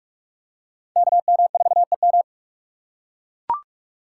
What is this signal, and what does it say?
Morse code, heard as a keyed tone, about 43 wpm, 690 Hz. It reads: KM4EM
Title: USA Call Signs - 5 Characters - 2x Character Spacing - 45wpm